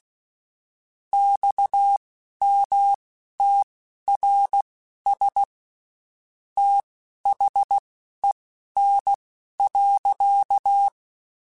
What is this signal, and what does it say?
Morse code, heard as a keyed tone, about 16 wpm, spaced as for 11 wpm, 800 Hz. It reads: XMTRS THEN.
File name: (spaced) / QSOs Spaced-mp3s / WA7QOX 1 @15 sp